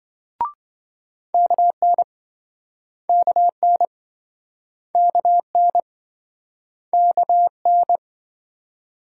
Morse code, heard as a keyed tone, about 28 wpm, 700 Hz. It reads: KN KN KN KN